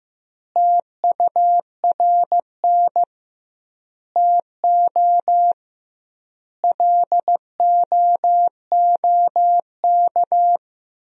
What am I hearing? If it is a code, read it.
Morse code, heard as a keyed tone, about 15 wpm, 700 Hz. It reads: TURN TO LOOK
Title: 2x Word-Spacing - Sentences from Top 200 Words - 15wpm